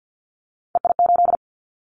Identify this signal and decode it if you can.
Morse code, heard as a keyed tone, about 50 wpm, 700 Hz. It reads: EI8
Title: Callsign Prefixes for Top 50 DXCC - 50wpm